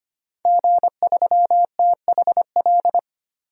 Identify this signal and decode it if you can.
Morse code, heard as a keyed tone, about 25 wpm, 700 Hz. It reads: G3THL
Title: DXCC-100 Call Signs - 5 Characters - 25wpm